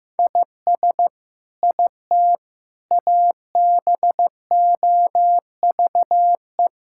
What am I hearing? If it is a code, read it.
Morse code, heard as a keyed tone, about 16 wpm, 700 Hz. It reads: IS IT ABOVE
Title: Sentences from Top 1000 Words - Review Part 5 - 15wpm